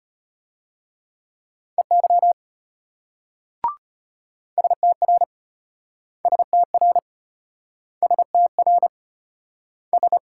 Morse code, heard as a keyed tone, about 32 wpm, 700 Hz. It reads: EY STR STR STR S